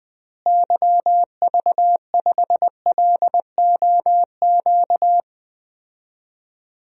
Morse code, heard as a keyed tone, about 20 wpm, 700 Hz. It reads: YV5LOQ
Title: DXCC-100 Call Signs - 6 Characters - Repeated 2x - 20wpm